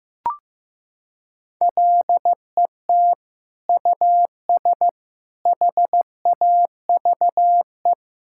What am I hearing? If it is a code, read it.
Morse code, heard as a keyed tone, about 16 wpm, 700 Hz. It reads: LET US HAVE
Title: Sentences from Top 1000 Words - Review Part 6 - 15wpm